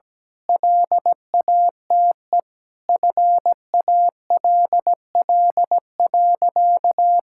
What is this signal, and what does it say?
Morse code, heard as a keyed tone, about 17 wpm, 700 Hz. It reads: LATE FALL.
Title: Sentences from Top 1000 Words - Review Part 6 - 17wpm